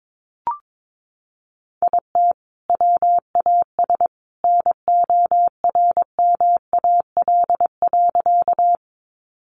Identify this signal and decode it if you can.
Morse code, heard as a keyed tone, about 22 wpm, 700 Hz. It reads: IT WAS NORMAL.
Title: Sentences from Top 1200 Words 22wpm